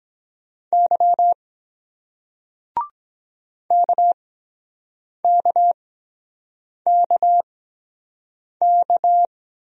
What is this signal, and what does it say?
Morse code, heard as a keyed tone, about 23 wpm, spaced as for 7 wpm, 700 Hz. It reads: Y KKKK